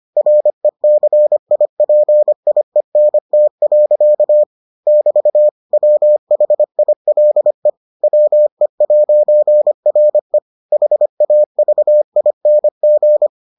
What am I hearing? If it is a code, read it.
Morse code, heard as a keyed tone, about 25 wpm, 600 Hz. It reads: RECIPIENT. <BT> WHILE WE'RE HAVING